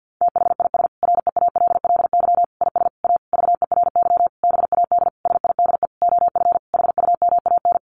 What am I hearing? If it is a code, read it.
Morse code, heard as a keyed tone, about 50 wpm, 700 Hz. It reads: THIS REALLY IS A VERY BAD SIDE OF HUMAN
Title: Sentences from Top 700 Words 50wpm